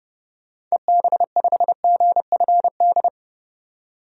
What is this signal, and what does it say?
Morse code, heard as a keyed tone, about 32 wpm, 700 Hz. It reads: EB5GFD
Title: DXCC-100 Call Signs - 6 Characters - 30wpm